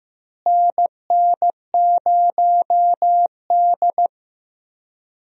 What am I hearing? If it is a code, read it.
Morse code, heard as a keyed tone, about 15 wpm, 700 Hz. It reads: NN0D
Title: USA Call Signs - 4 Characters - Repeated 2x - 15wpm